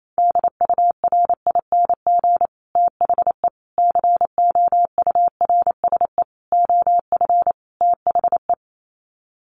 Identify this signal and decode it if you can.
Morse code, heard as a keyed tone, about 28 wpm, 700 Hz. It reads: DURING THE COURSE OF THE
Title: Sets of 5 Words Encore 28wpm